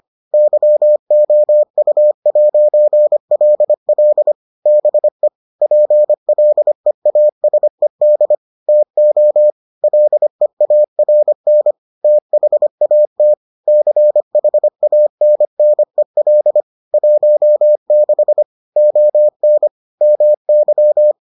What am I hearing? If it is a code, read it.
Morse code, heard as a keyed tone, about 25 wpm, 600 Hz. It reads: YOU'LL BE PLEASED TO LEARN THAT CHANNEL 16 ON MY